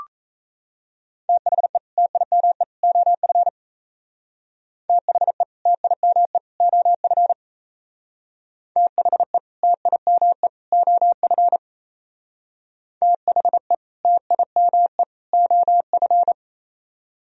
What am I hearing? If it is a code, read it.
Morse code, heard as a keyed tone, about 33 wpm, 700 Hz. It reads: THE TIME OF THE TIME OF THE TIME OF THE TIME OF
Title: Speed-Racing - Sets of 3 Words 28wpm